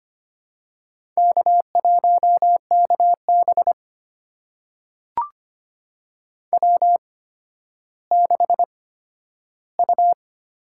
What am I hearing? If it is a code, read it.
Morse code, heard as a keyed tone, about 25 wpm, spaced as for 12 wpm, 700 Hz. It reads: K1KB W6U
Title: USA Call Signs - 4 Characters - 8x Character Spacing - 25wpm